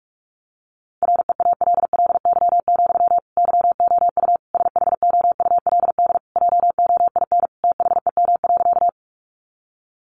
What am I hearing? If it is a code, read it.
Morse code, heard as a keyed tone, about 44 wpm, 700 Hz. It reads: REALLY, YOU SHOULD JOIN THEM.
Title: Sentences from Top 1400 Words 45wpm